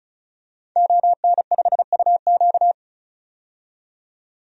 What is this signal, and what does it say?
Morse code, heard as a keyed tone, about 34 wpm, 700 Hz. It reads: ON5UQ
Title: DXCC-100 Call Signs - 5 Characters - 35wpm